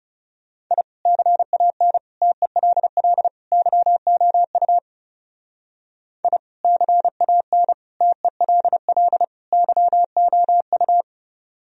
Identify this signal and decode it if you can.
Morse code, heard as a keyed tone, about 37 wpm, 700 Hz. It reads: I CAN TELL YOU I CAN TELL YOU
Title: Speed-Racing - Sets of 4 Words 30wpm